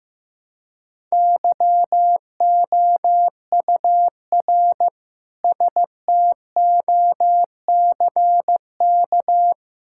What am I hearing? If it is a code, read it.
Morse code, heard as a keyed tone, about 15 wpm, 700 Hz. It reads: YOUR STOCK
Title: Sentences from Top 1000 Words - Review Part 5 - 15wpm